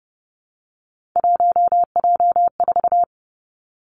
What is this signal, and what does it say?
Morse code, heard as a keyed tone, about 31 wpm, 700 Hz. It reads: 1J4
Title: Three Challenging Letter-Number Combinations 30wpm